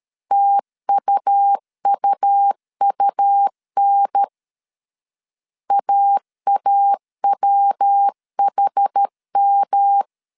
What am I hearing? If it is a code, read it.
Morse code, heard as a keyed tone, about 13 wpm, 800 Hz. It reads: TUUUN AAWHM
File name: CW / DET / 1 3 1500C S B 12wpm 2828